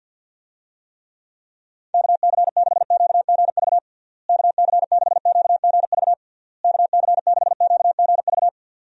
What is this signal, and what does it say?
Morse code, heard as a keyed tone, about 49 wpm, 700 Hz. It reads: KC6QGV KC6QGV KC6QGV
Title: USA Call Signs - 6 Characters - Repeated 3x - 50wpm